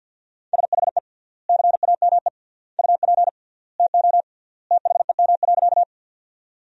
Morse code, heard as a keyed tone, about 46 wpm, 700 Hz. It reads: SHE CAME UP TO THEM.
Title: 3x Word-Spacing - Sentences from Top 200 Words - 50wpm